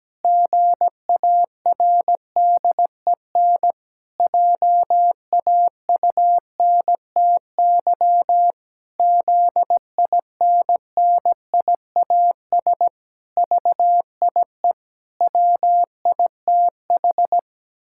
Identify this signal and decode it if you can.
Morse code, heard as a keyed tone, about 17 wpm, 700 Hz. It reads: GARDEN JAUNTY ZINNIAS VIE WITH
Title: Pangram Challenge v2 - 17wpm